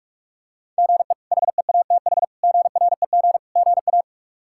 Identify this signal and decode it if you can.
Morse code, heard as a keyed tone, about 46 wpm, 700 Hz. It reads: GE HEATH GREG GA
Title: K1USN SST QSO Practice - 45wpm